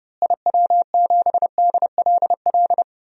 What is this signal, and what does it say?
Morse code, heard as a keyed tone, about 29 wpm, 700 Hz. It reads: IW7DLL
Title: DXCC-100 Call Signs - 6 Characters - Repeated 3x - 30wpm